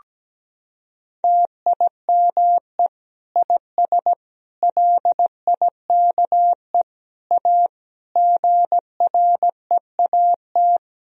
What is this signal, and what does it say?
Morse code, heard as a keyed tone, about 17 wpm, 700 Hz. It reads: TIME IS LIKE A GREAT
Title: Sentences from Top 1000 Words - Review Part 3 - 17wpm